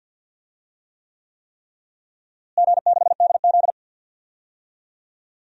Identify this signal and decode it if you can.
Morse code, heard as a keyed tone, about 51 wpm, 700 Hz. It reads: G6DZ
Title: DXCC-100 Call Signs - 4 Characters - Repeated 3x - 50wpm